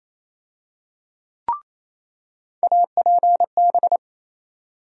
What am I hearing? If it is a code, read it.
Morse code, heard as a keyed tone, about 28 wpm, 700 Hz. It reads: APB
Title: USA Call Sign Postfixes - 3 Letters 28wpm